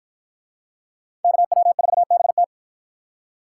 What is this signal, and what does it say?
Morse code, heard as a keyed tone, about 53 wpm, 700 Hz. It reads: KW4BT